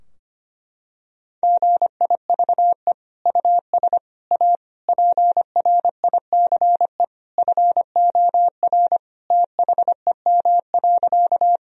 Morse code, heard as a keyed tone, about 25 wpm, 700 Hz. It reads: GIVE US A PRICE FOR THEM.
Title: Sentences from Top 1000 Words - Review Part 4 - 25wpm